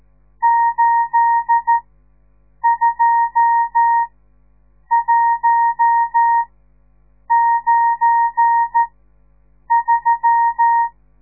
Morse code, heard as a keyed tone, about 14 wpm, spaced as for 7 wpm, 930 Hz. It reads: 82193
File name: radiomorningstar / morse / les 10